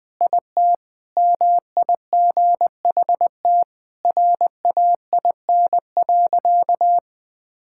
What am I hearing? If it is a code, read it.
Morse code, heard as a keyed tone, about 20 wpm, 700 Hz. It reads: IT MIGHT RAIN.
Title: Sentences with 3-Word Phrases - 20wpm